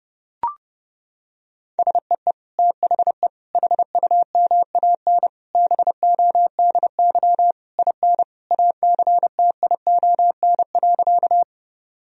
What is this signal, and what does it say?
Morse code, heard as a keyed tone, about 31 wpm, 700 Hz. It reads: SEE THE HUMAN BODY IN ACTION.
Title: Sentences from Top 1000 Words - Review Part 4 - 30wpm